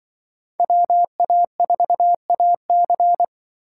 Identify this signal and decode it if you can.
Morse code, heard as a keyed tone, about 25 wpm, 700 Hz. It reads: WA4AC